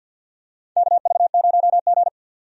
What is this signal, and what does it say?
Morse code, heard as a keyed tone, about 52 wpm, 700 Hz. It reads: KV0G
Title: USA Call Signs - 4 Characters - 50wpm